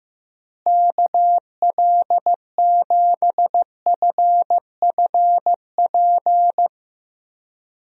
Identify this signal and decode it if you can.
Morse code, heard as a keyed tone, about 15 wpm, 700 Hz. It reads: KL7FFP
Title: USA Call Signs - 6 Characters - Repeated 3x - 15wpm